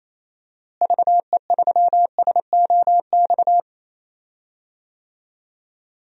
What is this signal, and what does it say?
Morse code, heard as a keyed tone, about 28 wpm, 700 Hz. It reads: VE3SOX
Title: DXCC-100 Call Signs - 6 Characters - Repeated 3x - 28wpm